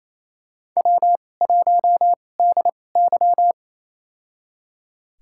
Morse code, heard as a keyed tone, about 28 wpm, spaced as for 20 wpm, 700 Hz. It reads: W1DY
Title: USA Call Signs - 4 Characters - 2x Character Spacing - 28wpm